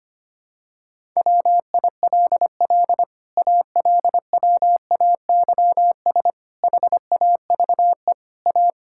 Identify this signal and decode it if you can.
Morse code, heard as a keyed tone, about 25 wpm, 700 Hz. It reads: WILL ALWAYS HAVE A
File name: Sets of 9 Words - 25wpm